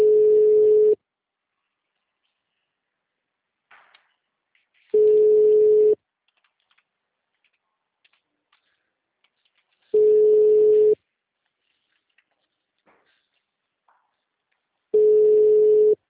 Ringback tone (Europe, 425 Hz).
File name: out- -2004- - - .65414